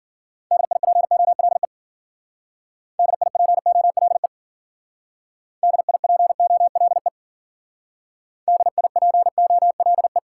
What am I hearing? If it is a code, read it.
Morse code, heard as a keyed tone, about 51 wpm, 700 Hz. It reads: DIPOLE DIPOLE DIPOLE DIPOLE